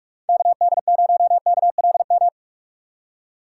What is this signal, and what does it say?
Morse code, heard as a keyed tone, about 42 wpm, 700 Hz. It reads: KD0KLM